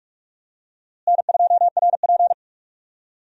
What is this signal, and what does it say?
Morse code, heard as a keyed tone, about 43 wpm, 700 Hz. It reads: N2RP
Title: USA Call Signs - 4 Characters - Repeated 3x - 45wpm